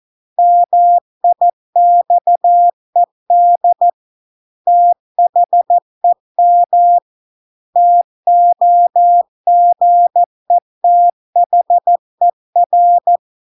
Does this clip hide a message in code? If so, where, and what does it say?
Yes: Morse code from 0.4 to 13.2 s, heard as a keyed tone, about 14 wpm, 700 Hz. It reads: MIXED THEM TOGETHER